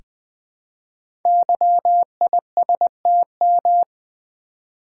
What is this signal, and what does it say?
Morse code, heard as a keyed tone, about 20 wpm, 700 Hz. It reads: YISTM
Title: Code Groups - 5 Letters - 20wpm